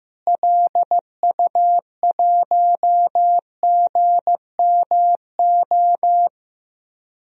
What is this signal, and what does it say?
Morse code, heard as a keyed tone, about 15 wpm, 700 Hz. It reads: LU1GMO
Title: DXCC-100 Call Signs - 6 Characters - Repeated 2x - 15wpm